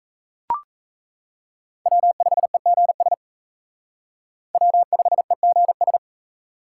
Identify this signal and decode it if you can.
Morse code, heard as a keyed tone, about 36 wpm, 700 Hz. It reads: W5EGS W5EGS